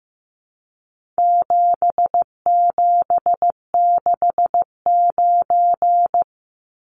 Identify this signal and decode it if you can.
Morse code, heard as a keyed tone, about 15 wpm, 700 Hz. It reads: 7769
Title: Numbers - 4-Digit 15wpm